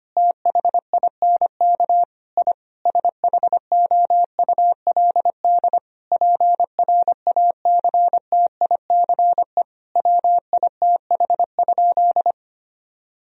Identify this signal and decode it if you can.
Morse code, heard as a keyed tone, about 25 wpm, 700 Hz. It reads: THINK I SHOULD PRACTICE WITH?
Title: Sentences from Top 1000 Words - Review Part 4 - 25wpm